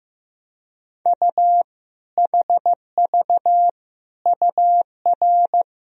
Morse code, heard as a keyed tone, about 15 wpm, 700 Hz. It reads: U HV UR
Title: Sentences from Top 400 Words and Abbreviations 15wpm